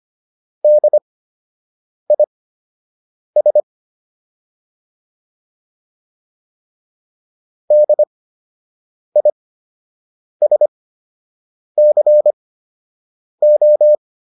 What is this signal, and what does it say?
Morse code, heard as a keyed tone, about 25 wpm, spaced as for 7 wpm, 600 Hz. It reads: DIS DISCO